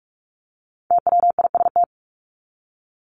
Morse code, heard as a keyed tone, about 44 wpm, 700 Hz. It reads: TWIST
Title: Top 1751-2000 Words 45wpm